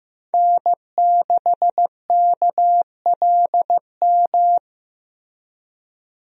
Morse code, heard as a keyed tone, about 15 wpm, 700 Hz. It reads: N6KLM
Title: USA Call Signs - 5 Characters - Repeated 2x - 15wpm